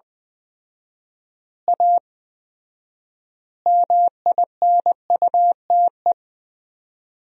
Morse code, heard as a keyed tone, about 20 wpm, 700 Hz. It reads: A MINUTE
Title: 4x Word-Spacing - Sentences from Top 500 Words - 20wpm